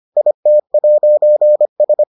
Morse code, heard as a keyed tone, about 25 wpm, 600 Hz. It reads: IT'S